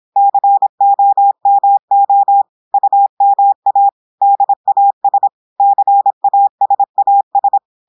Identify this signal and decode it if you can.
Morse code, heard as a keyed tone, about 26 wpm, 800 Hz. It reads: COMO UMA DAS CASAS
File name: audios / portugais / portugais texte en clair 1 vitesse 26 mots minutes